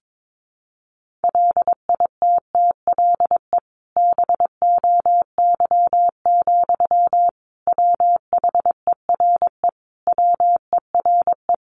Morse code, heard as a keyed tone, about 22 wpm, 700 Hz. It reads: LITTLE BOY, WHERE WERE
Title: Sentences from Top 1000 Words 22wpm